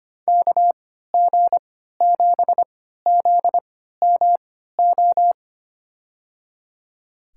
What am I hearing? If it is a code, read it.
Morse code, heard as a keyed tone, about 25 wpm, spaced as for 14 wpm, 700 Hz. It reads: KG7ZMO
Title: USA Call Signs - 6 Characters - 3x Character Spacing - 25wpm